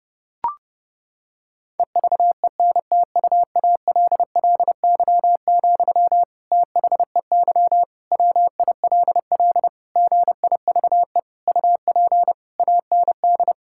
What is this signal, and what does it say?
Morse code, heard as a keyed tone, about 30 wpm, 700 Hz. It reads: EVENTUALLY, THEY WILL GIVE UP AND
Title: Sentences from Top 1000 Words - Review Part 6 - 30wpm